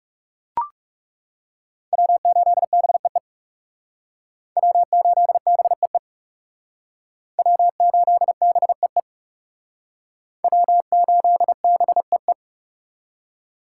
Morse code, heard as a keyed tone, about 40 wpm, 700 Hz. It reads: W8BEE W8BEE W8BEE W8BEE